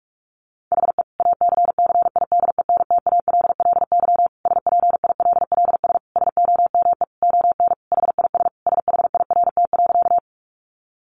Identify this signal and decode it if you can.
Morse code, heard as a keyed tone, about 44 wpm, 700 Hz. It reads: HE ACCIDENTALLY SPILLS SOME ON HIS SHIRT.
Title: Additional Easy Phrases 45wpm